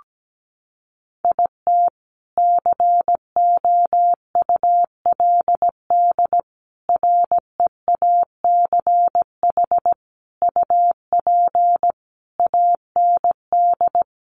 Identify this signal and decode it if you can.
Morse code, heard as a keyed tone, about 17 wpm, 700 Hz. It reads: IT COULD REACH UP AND
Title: A Sound Of Thunder 17wpm